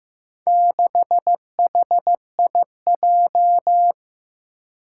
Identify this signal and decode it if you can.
Morse code, heard as a keyed tone, about 15 wpm, 700 Hz. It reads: 6HIJ
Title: Four Challenging Letter-Number Combinations 15wpm